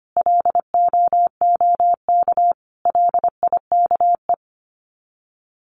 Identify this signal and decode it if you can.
Morse code, heard as a keyed tone, about 25 wpm, 700 Hz. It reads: LOOK LIKE
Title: Sets of 5 Words Encore2 25wpm